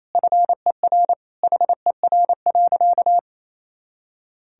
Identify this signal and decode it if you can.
Morse code, heard as a keyed tone, about 28 wpm, 700 Hz. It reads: FER HER.
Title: Sentences from Top 400 Words and Abbreviations 28wpm